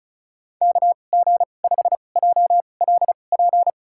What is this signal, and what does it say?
Morse code, heard as a keyed tone, about 36 wpm, spaced as for 25 wpm, 700 Hz. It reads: KG5JLP
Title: USA Call Signs - 6 Characters - 2x Character Spacing - 35wpm